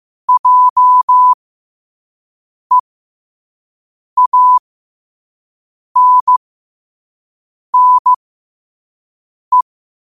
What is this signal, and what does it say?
Morse code, heard as a keyed tone, about 15 wpm, spaced as for 5 wpm, 1000 Hz. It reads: JEANNE
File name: qotdfiles / 5wpm